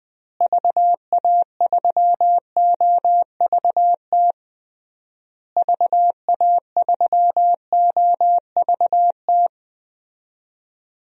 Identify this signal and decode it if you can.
Morse code, heard as a keyed tone, about 20 wpm, 700 Hz. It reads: VA3OVT VA3OVT
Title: DXCC-100 Call Signs - 6 Characters - Repeated 3x - 20wpm